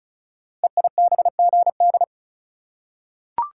Morse code, heard as a keyed tone, about 36 wpm, 700 Hz. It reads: EIBGD
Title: Code Groups - 5 Letters - 35wpm